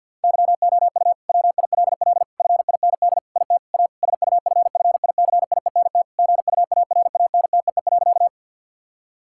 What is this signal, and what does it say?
Morse code, heard as a keyed tone, about 50 wpm, 700 Hz. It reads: YOU WILL FIND IT A SUFFICIENT GUARANTEE.
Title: Sentences from Top 1600 Words 50wpm